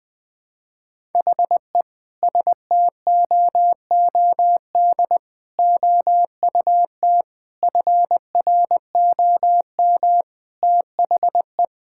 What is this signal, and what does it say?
Morse code, heard as a keyed tone, about 20 wpm, 700 Hz. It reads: HE STOOD OUT FROM THE
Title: Sentences with 5-Word Phrases - 20wpm